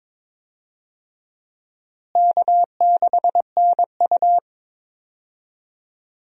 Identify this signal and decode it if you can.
Morse code, heard as a keyed tone, about 22 wpm, 700 Hz. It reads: K6NU
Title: USA Call Signs - 4 Characters - Repeated 2x - 22wpm